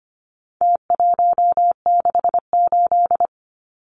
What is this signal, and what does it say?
Morse code, heard as a keyed tone, about 25 wpm, 700 Hz. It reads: T168
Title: Numbers - Contest Style - 4-Digit 25wpm